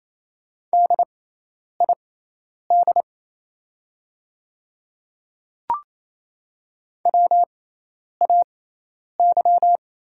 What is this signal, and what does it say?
Morse code, heard as a keyed tone, about 28 wpm, spaced as for 10 wpm, 700 Hz. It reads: DID WAY
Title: 6x Character Spacing - Top 200 Words - 28wpm